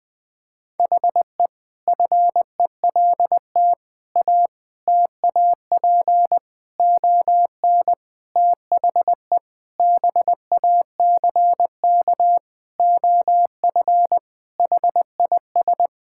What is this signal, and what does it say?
Morse code, heard as a keyed tone, about 20 wpm, 700 Hz. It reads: HE FELT A TAP ON THE BACK OF HIS
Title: Sentences with 4-Word Phrases - 20wpm